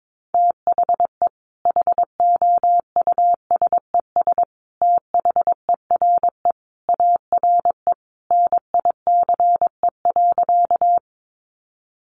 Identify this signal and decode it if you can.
Morse code, heard as a keyed tone, about 22 wpm, 700 Hz. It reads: THE HOUSES THERE ARE NICE.
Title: Easy Phrases 22wpm